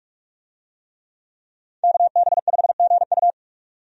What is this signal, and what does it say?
Morse code, heard as a keyed tone, about 46 wpm, 700 Hz. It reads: KB5GU
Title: USA Call Signs - 5 Characters - 45wpm